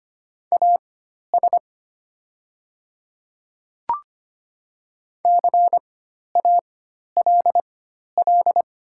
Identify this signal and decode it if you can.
Morse code, heard as a keyed tone, about 25 wpm, spaced as for 12 wpm, 700 Hz. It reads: AS CALL
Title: 4x Character Spacing - Top 300 Words - 25wpm